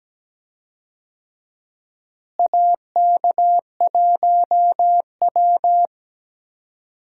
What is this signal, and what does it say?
Morse code, heard as a keyed tone, about 17 wpm, 700 Hz. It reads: AK1W
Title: USA Call Signs - 4 Characters - Repeated 3x - 17wpm